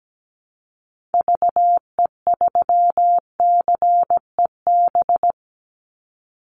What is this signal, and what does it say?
Morse code, heard as a keyed tone, about 17 wpm, 700 Hz. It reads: VE3CEB
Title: Call Signs - 17wpm